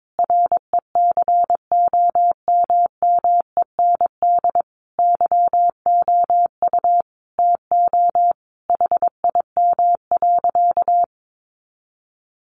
Morse code, heard as a keyed tone, about 22 wpm, 700 Hz. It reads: RECOMMEND YOU TO HIM.
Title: Sentences from Top 1200 Words 22wpm